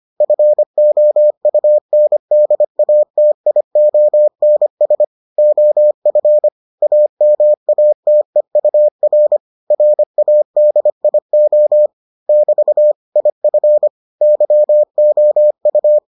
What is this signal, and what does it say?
Morse code, heard as a keyed tone, about 25 wpm, 600 Hz. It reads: FOUNDATIONS OF AMATEUR RADIO <BT> IF YOU